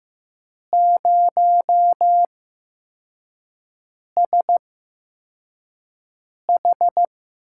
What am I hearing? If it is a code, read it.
Morse code, heard as a keyed tone, about 15 wpm, spaced as for 4 wpm, 700 Hz. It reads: 0SH
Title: USA Call Signs - 4 Characters - 8x Character Spacing - 15wpm